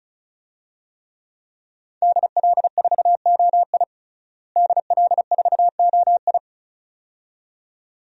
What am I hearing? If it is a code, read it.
Morse code, heard as a keyed tone, about 34 wpm, 700 Hz. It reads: DL4OI DL4OI
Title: USA Call Signs - 5 Characters - Repeated 2x - 35wpm